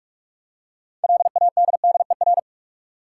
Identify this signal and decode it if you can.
Morse code, heard as a keyed tone, about 46 wpm, 700 Hz. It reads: LADDER
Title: Six Letter Words - 45wpm